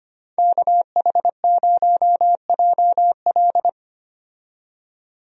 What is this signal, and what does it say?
Morse code, heard as a keyed tone, about 25 wpm, 700 Hz. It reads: KH0JL
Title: USA Call Signs - 5 Characters - 25wpm